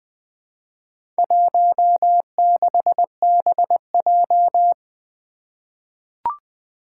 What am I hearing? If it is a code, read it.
Morse code, heard as a keyed tone, about 20 wpm, 700 Hz. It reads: 16BJ
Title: Four Challenging Letter-Number Combinations 20wpm